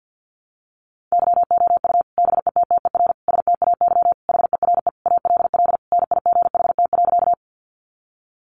Morse code, heard as a keyed tone, about 50 wpm, 700 Hz. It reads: YOU BETTER STAY HERE ALL NIGHT.
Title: Sentences from Top 500 Words 50wpm